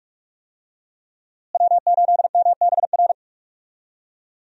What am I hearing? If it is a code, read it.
Morse code, heard as a keyed tone, about 42 wpm, 700 Hz. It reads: W8MBR
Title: USA Call Signs - 5 Characters - Repeated 2x - 45wpm